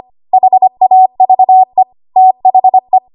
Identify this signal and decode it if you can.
Morse code, heard as a keyed tone, about 25 wpm, 750 Hz. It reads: HAVE THE
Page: 25 WPM Code Archive